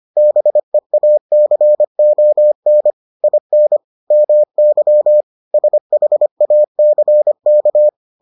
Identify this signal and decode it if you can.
Morse code, heard as a keyed tone, about 25 wpm, 600 Hz. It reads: BEACON IN MY SHACK